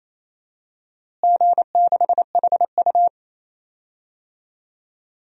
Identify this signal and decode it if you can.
Morse code, heard as a keyed tone, about 28 wpm, 700 Hz. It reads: G6HU
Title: DXCC-100 Call Signs - 4 Characters - Repeated 2x - 28wpm